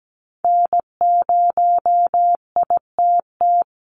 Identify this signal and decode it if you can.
Morse code, heard as a keyed tone, about 17 wpm, 700 Hz. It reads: N0ITT
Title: USA Call Signs 17wpm